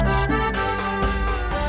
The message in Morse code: none